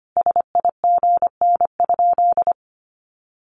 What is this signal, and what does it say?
Morse code, heard as a keyed tone, about 25 wpm, 700 Hz. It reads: SIGN?
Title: Sentences from Top 1100 Words 25wpm